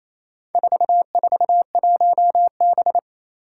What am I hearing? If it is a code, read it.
Morse code, heard as a keyed tone, about 28 wpm, 700 Hz. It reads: 441B
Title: Four Challenging Letter-Number Combinations 28wpm